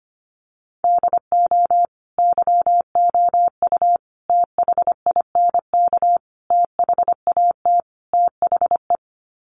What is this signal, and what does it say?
Morse code, heard as a keyed tone, about 25 wpm, 700 Hz. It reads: DO YOU THINK THAT THE
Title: Sets of 5 Words Encore 25wpm